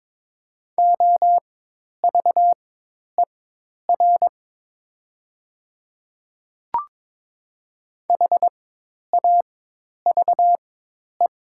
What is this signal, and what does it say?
Morse code, heard as a keyed tone, about 22 wpm, spaced as for 10 wpm, 700 Hz. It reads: OVER HAVE
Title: 4x Character Spacing - Top 100 Words - 22wpm